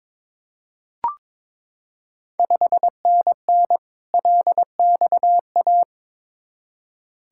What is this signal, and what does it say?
Morse code, heard as a keyed tone, about 22 wpm, 700 Hz. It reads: 5NN LXA
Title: Ontario QSO Party Exchange 22wpm